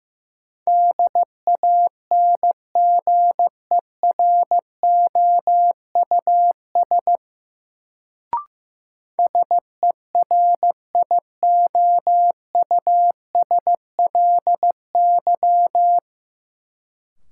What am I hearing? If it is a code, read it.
Morse code, heard as a keyed tone, about 15 wpm, 700 Hz. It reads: DANGEROUS SERIOUSLY
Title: Sentences from Top 1200 Words - New Word Drill - 15wpm